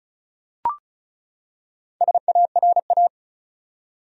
Morse code, heard as a keyed tone, about 36 wpm, 700 Hz. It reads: SARA
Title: USA Common Names - Max 5 Letters - 35wpm